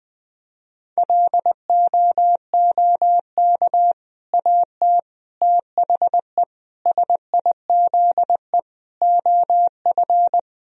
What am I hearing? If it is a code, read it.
Morse code, heard as a keyed tone, about 20 wpm, 700 Hz. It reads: LOOK AT THE SIZE OF
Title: Sentences from Top 1000 Words - Review Part 2 - 20wpm